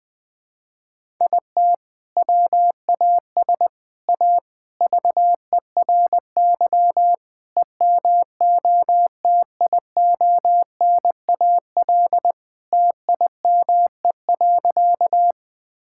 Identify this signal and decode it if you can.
Morse code, heard as a keyed tone, about 19 wpm, 700 Hz. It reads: IT WAS A VERY EMOTIONAL TIME.
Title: Sentences from Top 1000 Words - Review Part 5 - 20wpm